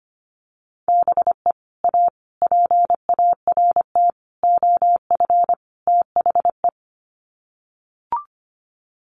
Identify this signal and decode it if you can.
Morse code, heard as a keyed tone, about 25 wpm, 700 Hz. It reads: BE A PART OF THE
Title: Sets of 5 Words Encore 25wpm